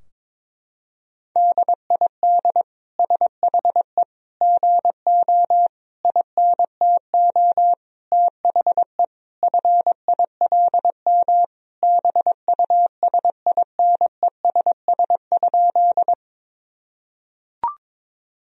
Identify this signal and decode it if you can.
Morse code, heard as a keyed tone, about 22 wpm, 700 Hz. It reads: DID SHE GO INTO THE FILM BUSINESS?
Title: Sentences from Top 1000 Words - Review Part 3 - 22wpm